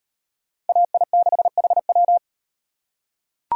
Morse code, heard as a keyed tone, about 38 wpm, 700 Hz. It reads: AI6HW